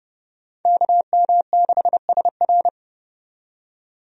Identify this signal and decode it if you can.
Morse code, heard as a keyed tone, about 30 wpm, 700 Hz. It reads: KM6SR
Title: USA Call Signs - 5 Characters - 30wpm